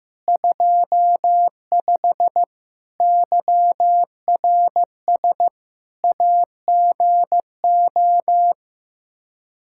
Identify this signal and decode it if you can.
Morse code, heard as a keyed tone, about 15 wpm, 700 Hz. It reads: 25 YRS AGO
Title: Practice QSO - Exchange 3 - More Info - 15wpm